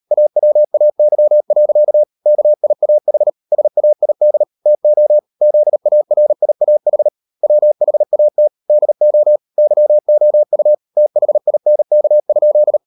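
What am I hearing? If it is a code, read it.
Morse code, heard as a keyed tone, about 38 wpm, 600 Hz. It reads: AWAY. KIAH SAID TO ZARIAH WHAT DO YOU THINK?
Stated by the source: Morse code practice at 38 WPM, sending Ken Follett's novel Never in English, from CW med Gnister.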